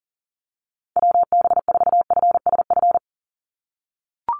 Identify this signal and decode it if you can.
Morse code, heard as a keyed tone, about 40 wpm, 700 Hz. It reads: WB4FSF
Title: Call Signs - 40wpm